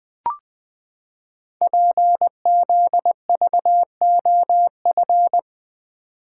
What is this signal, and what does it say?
Morse code, heard as a keyed tone, about 20 wpm, 700 Hz. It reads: PZVOF
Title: Code Groups - 5 Letters - 20wpm